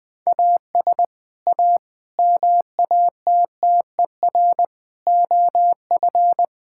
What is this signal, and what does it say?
Morse code, heard as a keyed tone, about 20 wpm, 700 Hz. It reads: AS A MATTER OF
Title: Sentences with 4-Word Phrases - Spoken Beginning - 20wpm